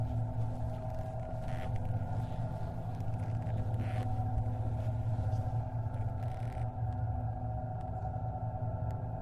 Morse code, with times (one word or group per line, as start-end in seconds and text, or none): none